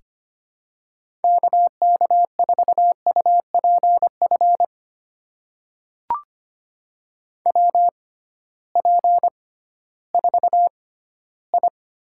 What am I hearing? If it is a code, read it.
Morse code, heard as a keyed tone, about 25 wpm, spaced as for 15 wpm, 700 Hz. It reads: KK4UPF WP4I
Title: USA Call Signs - 6x Character Spacing - 25wpm